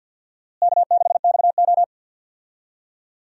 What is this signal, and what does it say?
Morse code, heard as a keyed tone, about 50 wpm, 700 Hz. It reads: K6XO
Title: USA Call Signs - 4 Characters - Repeated 3x - 50wpm